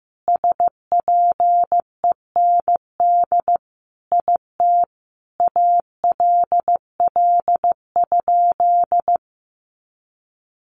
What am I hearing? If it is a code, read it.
Morse code, heard as a keyed tone, about 15 wpm, 700 Hz. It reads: SPEND IT ALL?
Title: Sentences from Top 800 Words 15wpm